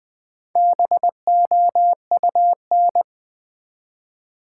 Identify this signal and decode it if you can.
Morse code, heard as a keyed tone, about 20 wpm, 700 Hz. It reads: BOUN
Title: Build A Word - 1 Letter At A Time - 5-Letter Words - 20wpm